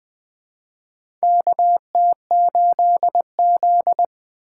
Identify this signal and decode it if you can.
Morse code, heard as a keyed tone, about 20 wpm, 700 Hz. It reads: KT8Z
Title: USA Call Signs - 3x Character Spacing - 20wpm